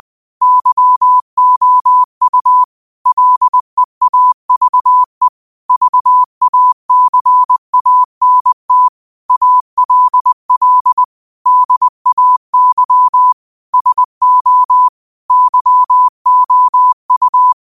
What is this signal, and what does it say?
Morse code, heard as a keyed tone, about 20 wpm, 1000 Hz. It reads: YOU LEAVE VACANT ALL DAY SO YOU
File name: 20wpm